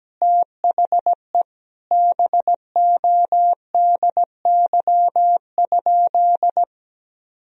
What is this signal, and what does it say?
Morse code, heard as a keyed tone, about 17 wpm, 700 Hz. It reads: THE BODY?
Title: Sentences from Top 400 Words and Abbreviations 17wpm